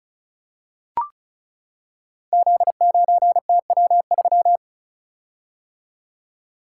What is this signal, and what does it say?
Morse code, heard as a keyed tone, about 35 wpm, 700 Hz. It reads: Z9TW3
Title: Code Groups - 5 Letter:Numbers - 35wpm